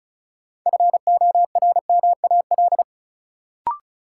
Morse code, heard as a keyed tone, about 36 wpm, 700 Hz. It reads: FORMAL
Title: Sentences from Top 1300 Words - New Word Drill - 35wpm